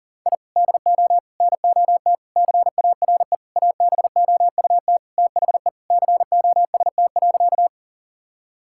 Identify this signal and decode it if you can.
Morse code, heard as a keyed tone, about 40 wpm, 700 Hz. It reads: I DO NOT CARE ABOUT THE COST.
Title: Sentences from Top 1000 Words - Review Part 5 - 40wpm